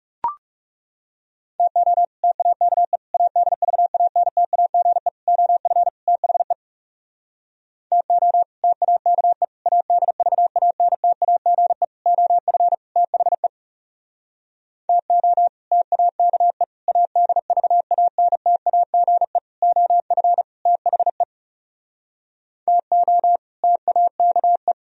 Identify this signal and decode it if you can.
Morse code, heard as a keyed tone, about 40 wpm, 700 Hz. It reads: TO TAKE ADVANTAGE OF THE TO TAKE ADVANTAGE OF THE TO TAKE ADVANTAGE OF THE TO TAKE